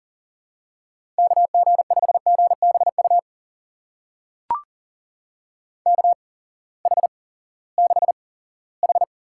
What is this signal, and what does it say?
Morse code, heard as a keyed tone, about 38 wpm, spaced as for 20 wpm, 700 Hz. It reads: KG5GBU KH6H
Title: USA Call Signs - 8x Character Spacing - 40wpm